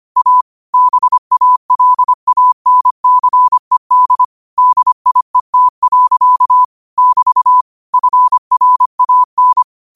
Morse code, heard as a keyed tone, about 25 wpm, 1000 Hz. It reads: A BALANCED DIET. <BT> FRAN
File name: qotdfiles / 25wpm